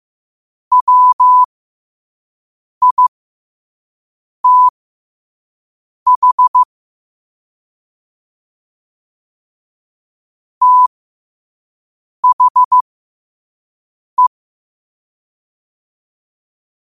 Morse code, heard as a keyed tone, about 15 wpm, spaced as for 5 wpm, 1000 Hz. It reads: WITH THE
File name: qotdfiles / 5wpm